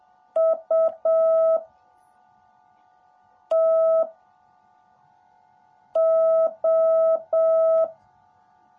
Morse code, heard as a keyed tone, about 7 wpm, 650 Hz. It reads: U T O
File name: jh8cbh / 2021cw / cw18a